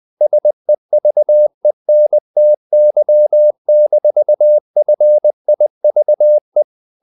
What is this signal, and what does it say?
Morse code, heard as a keyed tone, about 20 wpm, 600 Hz. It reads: SEVENTY-FIVE
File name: the old man and the sea / the old man and the sea part 1 20wpm